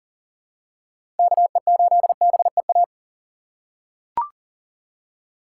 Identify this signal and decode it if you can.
Morse code, heard as a keyed tone, about 37 wpm, 700 Hz. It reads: KE8BEA
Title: USA Call Signs - 6 Characters - 2x Character Spacing - 40wpm